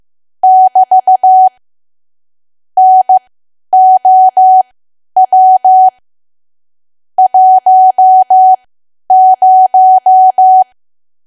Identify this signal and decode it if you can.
Morse code, heard as a keyed tone, about 15 wpm, spaced as for 10 wpm, 750 Hz. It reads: <BT> NOW 10
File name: Archive / 10 WPM / 10WPM